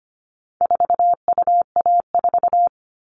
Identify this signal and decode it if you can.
Morse code, heard as a keyed tone, about 25 wpm, 700 Hz. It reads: 4UA4
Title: All Cut Numbers - 4-Digit 25wpm